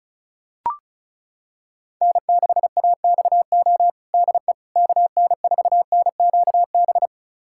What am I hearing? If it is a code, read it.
Morse code, heard as a keyed tone, about 36 wpm, 700 Hz. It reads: N6AXO DE KN4NQB
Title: USA Callsign DE Callsign 35wpm